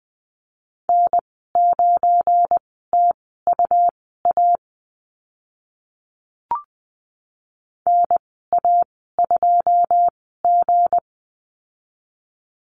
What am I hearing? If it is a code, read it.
Morse code, heard as a keyed tone, about 20 wpm, spaced as for 14 wpm, 700 Hz. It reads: N9TUA NA2G
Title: 245 Lesson 39g - Letter z - Call Signs 20wpm